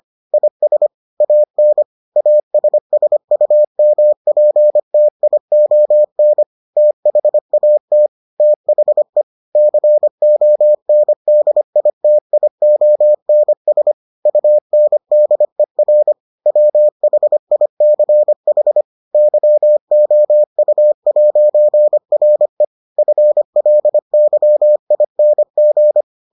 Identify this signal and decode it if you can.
Morse code, heard as a keyed tone, about 25 wpm, 600 Hz. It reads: IS AN ASSUMPTION THAT THE CONDITIONS UNDER WHICH YOU'RE FLYING